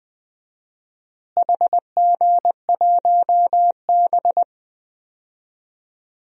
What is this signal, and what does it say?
Morse code, heard as a keyed tone, about 20 wpm, 700 Hz. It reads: HG1B
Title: DXCC-100 Call Signs - 4 Characters - Repeated 3x - 20wpm